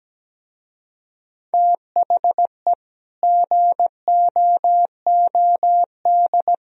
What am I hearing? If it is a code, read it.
Morse code, heard as a keyed tone, about 17 wpm, 700 Hz. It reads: THE GOOD
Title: Speed-Racing - Sets of 5 Words 17wpm